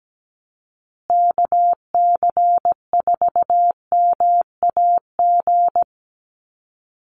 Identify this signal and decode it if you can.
Morse code, heard as a keyed tone, about 17 wpm, 700 Hz. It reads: KC4MAG
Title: USA Call Signs Repeated 3x 17wpm